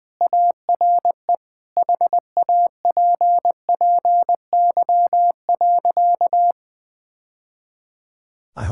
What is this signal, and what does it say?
Morse code, heard as a keyed tone, about 20 wpm, 700 Hz. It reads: ARE HAPPY.
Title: Sentences with 3-Word Phrases - Spoken Beginning - 20wpm